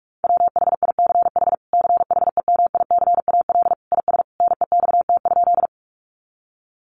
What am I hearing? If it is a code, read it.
Morse code, heard as a keyed tone, about 44 wpm, 700 Hz. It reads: WHICH CHEMICAL IS NEXT?
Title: Sentences from Top 1200 Words 45wpm